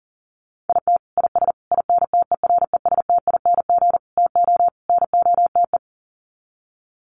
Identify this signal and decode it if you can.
Morse code, heard as a keyed tone, about 40 wpm, 700 Hz. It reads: IT IS INTERESTING TO NOTE
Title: Sets of 5 Words Encore2 40wpm